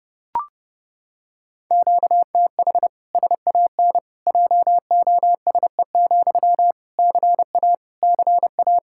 Morse code, heard as a keyed tone, about 29 wpm, 700 Hz. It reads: QTH SAN JOSE, CA CA
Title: Practice QSO - Exchange 1 - Beginning - 30wpm